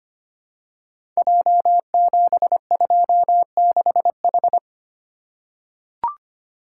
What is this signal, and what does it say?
Morse code, heard as a keyed tone, about 25 wpm, 700 Hz. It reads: J726H
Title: Code Groups - 5 Letter:Numbers - 25wpm